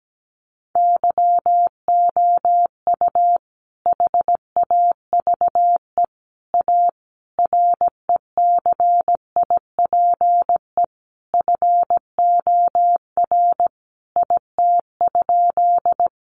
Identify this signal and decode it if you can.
Morse code, heard as a keyed tone, about 17 wpm, 700 Hz. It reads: YOU HAVE A RECIPE FOR IT?
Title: Sentences from Top 1200 Words 17wpm